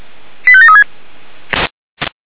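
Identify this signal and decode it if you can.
Five-tone selective call (ZVEI), heard as quick runs of five tones, tones of about 70 ms. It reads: 85736
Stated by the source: ZVEI five-tone call 85371 received by SDR, channel 464.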